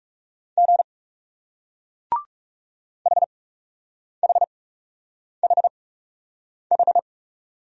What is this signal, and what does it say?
Morse code, heard as a keyed tone, about 40 wpm, spaced as for 8 wpm, 700 Hz. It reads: G HHHH